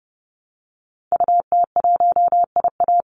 Morse code, heard as a keyed tone, about 30 wpm, 700 Hz. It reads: UT1IA
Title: Call Signs - 30wpm